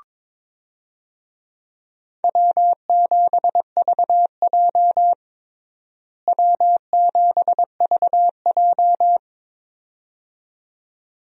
Morse code, heard as a keyed tone, about 22 wpm, 700 Hz. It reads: W7VJ W7VJ
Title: USA Call Signs - 4 Characters - Repeated 2x - 22wpm